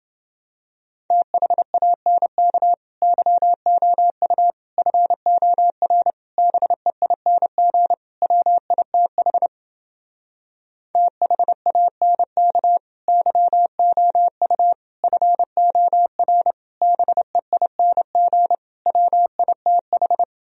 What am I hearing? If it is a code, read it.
Morse code, heard as a keyed tone, about 30 wpm, 700 Hz. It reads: THANK YOU FOR BEING WITH THANK YOU FOR BEING WITH